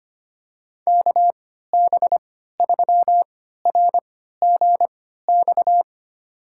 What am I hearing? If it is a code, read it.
Morse code, heard as a keyed tone, about 25 wpm, spaced as for 14 wpm, 700 Hz. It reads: KB3RGX
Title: USA Call Signs - 3x Character Spacing - 25wpm